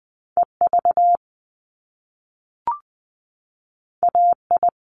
Morse code, heard as a keyed tone, about 20 wpm, 700 Hz. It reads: EV AI
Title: Top Two Letter Combinations in Words 20wpm